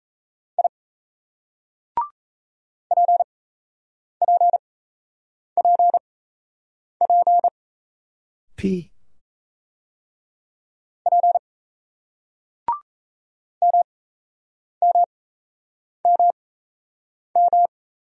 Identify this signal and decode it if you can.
Morse code, heard as a keyed tone, about 35 wpm, spaced as for 8 wpm, 700 Hz. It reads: I PPPP P MMMM
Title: Speed-Racing Single Letters 28wpm